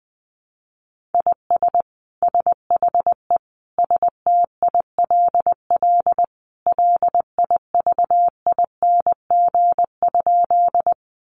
Morse code, heard as a keyed tone, about 20 wpm, 700 Hz. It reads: IS SHE STILL LIVING?
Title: Sentences from Top 1100 Words 20wpm